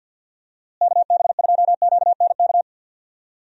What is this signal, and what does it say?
Morse code, heard as a keyed tone, about 50 wpm, 700 Hz. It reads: KB2QNK
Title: USA Call Signs - 6 Characters - Repeated 3x - 50wpm